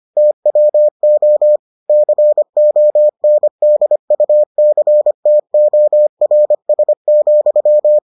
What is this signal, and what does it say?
Morse code, heard as a keyed tone, about 25 wpm, 600 Hz. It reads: TWO CONDUCTORS,